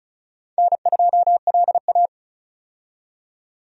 Morse code, heard as a keyed tone, about 36 wpm, 700 Hz. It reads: N2LA
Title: USA Call Signs - 4 Characters - 35wpm